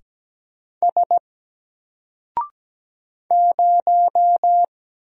Morse code, heard as a keyed tone, about 17 wpm, 700 Hz. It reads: S 0
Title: Single Letter-Number 17wpm